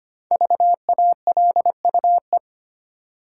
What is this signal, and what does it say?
Morse code, heard as a keyed tone, about 25 wpm, 700 Hz. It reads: VALUE
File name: 5 Letter Words - First 3 Letters Given - 25wpm